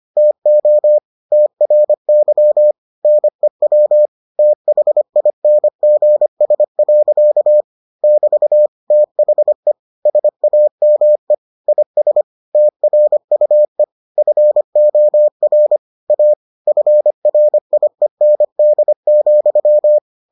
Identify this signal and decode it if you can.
Morse code, heard as a keyed tone, about 25 wpm, 600 Hz. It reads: TO TRY NEW THINGS. <BT> THE SAME IS TRUE FOR A FRIEND,